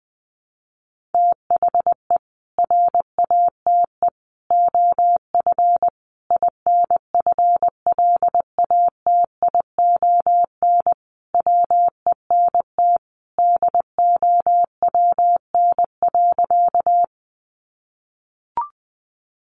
Morse code, heard as a keyed tone, about 20 wpm, 700 Hz. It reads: THE RATE OF INFLATION WENT DOWN.
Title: Sentences from Top 1100 Words 20wpm